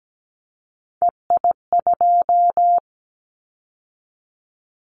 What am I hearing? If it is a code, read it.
Morse code, heard as a keyed tone, about 17 wpm, 700 Hz. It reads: EI2
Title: Callsign Prefixes for Top 50 DXCC - 17wpm